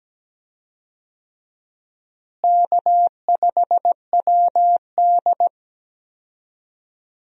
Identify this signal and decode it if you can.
Morse code, heard as a keyed tone, about 17 wpm, 700 Hz. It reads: K5WD
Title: USA Call Signs - 4 Characters - Repeated 3x - 17wpm